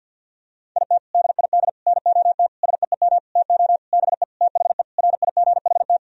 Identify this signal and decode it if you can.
Morse code, heard as a keyed tone, about 52 wpm, 700 Hz. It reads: IT DID NOT SEEM TO BE THE RIGHT
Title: Sentences from Top 1000 Words - Review Part 2 - 50wpm